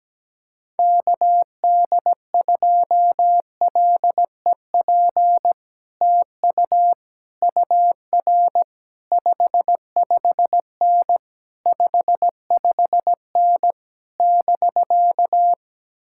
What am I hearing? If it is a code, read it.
Morse code, heard as a keyed tone, about 17 wpm, 700 Hz. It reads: KD2LEP TU UR 55N 55N <BK>
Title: Speed-Racing Common QSO Phrases - 17wpm